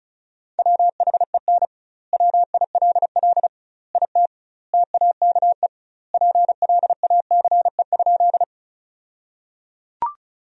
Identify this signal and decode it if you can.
Morse code, heard as a keyed tone, about 34 wpm, 700 Hz. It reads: WHEN WILL IT TAKE PLACE?
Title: 2x Word-Spacing - Sentences from Top 200 Words - 35wpm